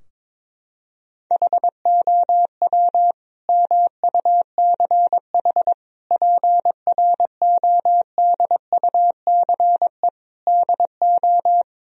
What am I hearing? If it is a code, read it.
Morse code, heard as a keyed tone, about 22 wpm, 700 Hz. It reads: HOW MUCH PRODUCE DO
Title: Sentences from Top 1000 Words - Review Part 3 - 22wpm